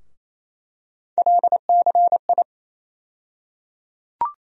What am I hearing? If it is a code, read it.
Morse code, heard as a keyed tone, about 28 wpm, 700 Hz. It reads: LCI
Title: USA Call Sign Postfixes - 3 Letters 28wpm